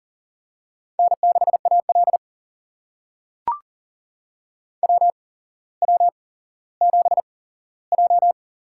Morse code, heard as a keyed tone, about 41 wpm, spaced as for 17 wpm, 700 Hz. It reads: N6AL WW7J
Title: USA Call Signs - 4 Characters - 8x Character Spacing - 40wpm